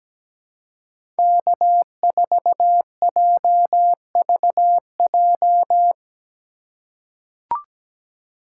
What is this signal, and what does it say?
Morse code, heard as a keyed tone, about 17 wpm, 700 Hz. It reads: K4JVJ
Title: USA Call Signs - 5 Characters - 3x Character Spacing - 17wpm